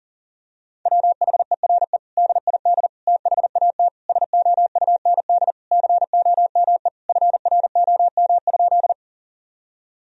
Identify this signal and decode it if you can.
Morse code, heard as a keyed tone, about 37 wpm, 700 Hz. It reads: WHERE DID THAT SOUND COME FROM?
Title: Sentences from Top 1000 Words - Review Part 1 - 40wpm